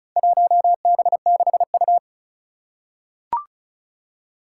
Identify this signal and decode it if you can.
Morse code, heard as a keyed tone, about 36 wpm, 700 Hz. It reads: 1B6U
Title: Four Challenging Letter-Number Combinations 35wpm